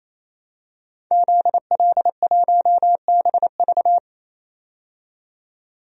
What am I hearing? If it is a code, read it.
Morse code, heard as a keyed tone, about 28 wpm, 700 Hz. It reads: ZL1BV
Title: DXCC-100 Call Signs - 5 Characters - 28wpm